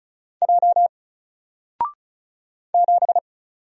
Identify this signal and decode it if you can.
Morse code, heard as a keyed tone, about 36 wpm, 700 Hz. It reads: J 7
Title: Single Character (Letter:Number:Punctuation:Prosign) Alt 35wpm